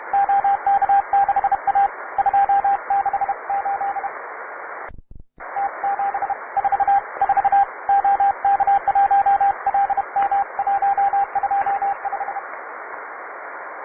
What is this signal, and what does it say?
Morse code, heard as a keyed tone, about 32 wpm, 800 Hz, with radio noise. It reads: OK6A 268 T744 OK1LM 125
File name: wae / cw advanced / tu2ci cw